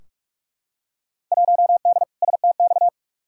Morse code, heard as a keyed tone, about 42 wpm, 700 Hz. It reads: 1D STX
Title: Field Day Exchanges - 45wpm